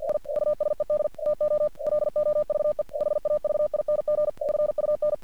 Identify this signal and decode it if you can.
Morse code, heard as a keyed tone, about 45 wpm, 600 Hz. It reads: I USED TO LOVE HAVING FUN
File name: archived / CRSnet